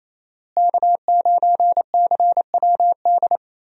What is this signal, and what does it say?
Morse code, heard as a keyed tone, about 27 wpm, 700 Hz. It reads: K9CWD
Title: USA Call Signs - 5 Characters - Repeated 2x - 28wpm